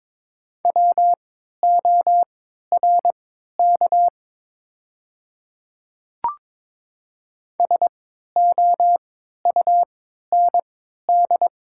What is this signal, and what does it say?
Morse code, heard as a keyed tone, about 22 wpm, spaced as for 13 wpm, 700 Hz. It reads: WORK SOUND
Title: 3x Character Spacing - Top 200 Words - 22wpm